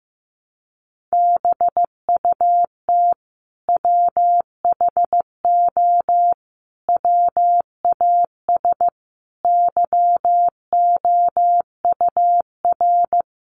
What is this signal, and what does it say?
Morse code, heard as a keyed tone, about 15 wpm, 700 Hz. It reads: BUT WHO WAS YOUR
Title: Sentences from Top 200 Words 15wpm